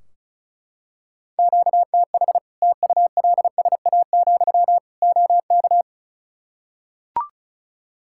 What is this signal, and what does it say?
Morse code, heard as a keyed tone, about 36 wpm, 700 Hz. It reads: QTH TULSA, OK
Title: Practice QSO - Exchange 1 - Beginning - 35wpm